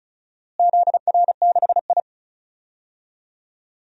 Morse code, heard as a keyed tone, about 34 wpm, 700 Hz. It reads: ZR6I
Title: DXCC-100 Call Signs - 4 Characters - Repeated 3x - 35wpm